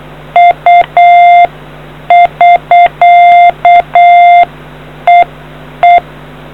Morse code, heard as a keyed tone, about 8 wpm, 700 Hz. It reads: U <SK> EE